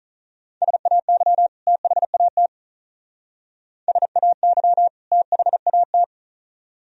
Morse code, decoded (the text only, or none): SAY THAT SAY THAT